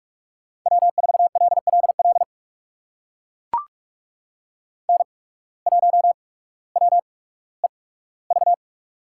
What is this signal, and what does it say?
Morse code, heard as a keyed tone, about 44 wpm, spaced as for 19 wpm, 700 Hz. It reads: W4LLL N1WEV
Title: USA Call Signs - 5 Characters - 8x Character Spacing - 45wpm